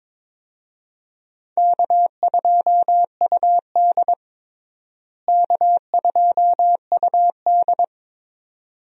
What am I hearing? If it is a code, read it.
Morse code, heard as a keyed tone, about 22 wpm, 700 Hz. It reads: K2UD K2UD
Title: USA Call Signs - 4 Characters - Repeated 2x - 22wpm